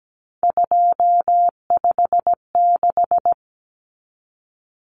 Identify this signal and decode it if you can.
Morse code, heard as a keyed tone, about 17 wpm, 690 Hz. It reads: 256
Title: Common Cut Numbers - 3-Digit 17wpm